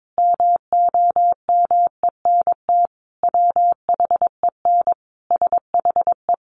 Morse code, heard as a keyed tone, about 22 wpm, 700 Hz. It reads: MOMENT WHEN SHE
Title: Sentences from Top 1000 Words 22wpm